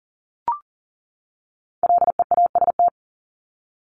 Morse code, heard as a keyed tone, about 40 wpm, 700 Hz. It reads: LEAST
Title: Top 501-750 Words 40wpm